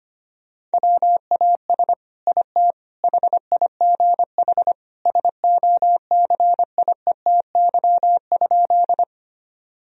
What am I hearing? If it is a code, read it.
Morse code, heard as a keyed tone, about 25 wpm, 700 Hz. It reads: WAS IT HIGH SOCIETY?
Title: Sentences from Top 1000 Words - Review Part 4 - 25wpm